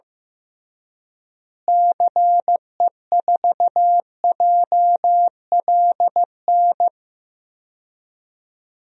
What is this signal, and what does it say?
Morse code, heard as a keyed tone, about 15 wpm, 700 Hz. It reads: CE4JLN
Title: DXCC-100 Call Signs - 6 Characters - Repeated 3x - 15wpm